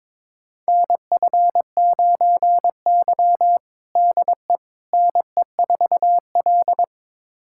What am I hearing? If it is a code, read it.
Morse code, heard as a keyed tone, about 21 wpm, 700 Hz. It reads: NF9Y DE NE4L
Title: USA Callsign DE Callsign - 4 Characters - 22wpm